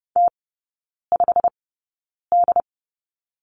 Morse code, heard as a keyed tone, about 30 wpm, spaced as for 9 wpm, 700 Hz. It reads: T5D
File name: course / 30:10wpm / 087 Lesson 15e - Number 5 - Three Characters 30wpm